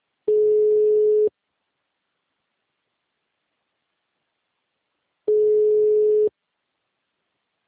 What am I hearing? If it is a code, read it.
Ringback tone (Europe, 425 Hz).
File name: out- -2013- - - .178